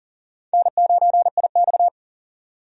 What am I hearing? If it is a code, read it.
Morse code, heard as a keyed tone, about 40 wpm, 700 Hz. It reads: N9IX
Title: USA Call Signs - 4 Characters - 40wpm